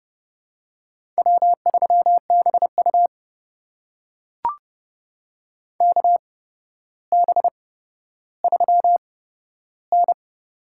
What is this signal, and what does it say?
Morse code, heard as a keyed tone, about 30 wpm, spaced as for 13 wpm, 700 Hz. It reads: W3BU KB3N
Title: USA Call Signs - 4 Characters - 8x Character Spacing - 30wpm